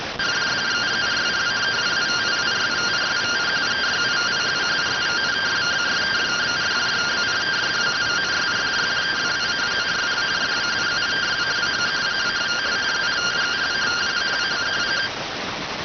Two-tone teleprinter signal, about 170 Hz shift.